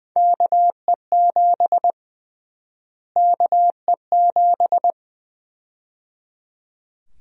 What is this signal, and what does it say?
Morse code, heard as a keyed tone, about 20 wpm, 700 Hz. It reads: KE7 KE7
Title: USA Call Sign Prefixes - Repeated 2x - 20wpm